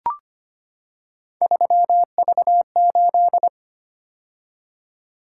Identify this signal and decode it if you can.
Morse code, heard as a keyed tone, about 25 wpm, 700 Hz. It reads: 3V8
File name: Three Letter-Number Combinations - 25wpm